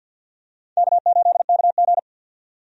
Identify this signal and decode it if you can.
Morse code, heard as a keyed tone, about 47 wpm, 700 Hz. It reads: K8KG
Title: USA Call Signs - 4 Characters - 50wpm